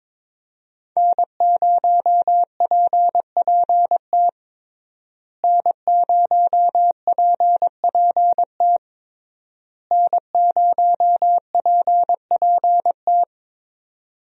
Morse code, heard as a keyed tone, about 22 wpm, 700 Hz. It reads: N0PPT N0PPT N0PPT
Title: USA Call Signs - 5 Characters - Repeated 3x - 22wpm